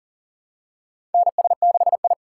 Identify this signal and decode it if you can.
Morse code, heard as a keyed tone, about 40 wpm, 700 Hz. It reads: NS6I
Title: USA Call Signs - 2x Character Spacing - 40wpm